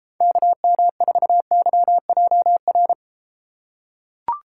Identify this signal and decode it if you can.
Morse code, heard as a keyed tone, about 32 wpm, 700 Hz. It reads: KM4YJR